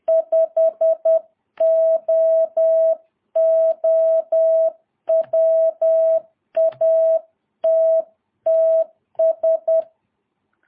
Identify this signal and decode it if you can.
Morse code, heard as a keyed tone, about 10 wpm, 650 Hz. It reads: 5OO WATTS
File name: jh8cbh / 2022cw